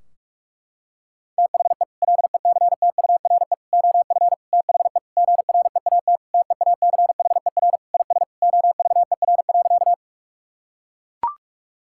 Morse code, heard as a keyed tone, about 41 wpm, 700 Hz. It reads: THE LECTURE OF THE GREAT TEACHER IS OVER.
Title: Sentences from Top 1600 Words 45wpm